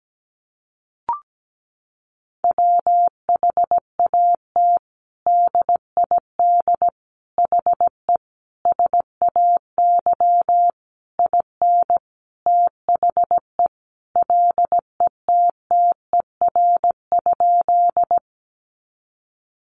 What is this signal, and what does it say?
Morse code, heard as a keyed tone, about 17 wpm, 700 Hz. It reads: WHAT DID HE SAY IN THE LETTER?
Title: Sentences from Top 300 Words 17wpm